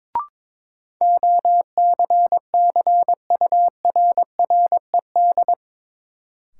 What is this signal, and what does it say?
Morse code, heard as a keyed tone, about 22 wpm, 700 Hz. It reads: OCCURRED
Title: Sets of 5 Words Encore - New Word Drill - 22wpm